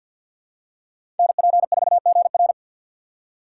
Morse code, heard as a keyed tone, about 51 wpm, 690 Hz. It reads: NP4GR
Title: USA Call Signs - 5 Characters - 3x Character Spacing - 50wpm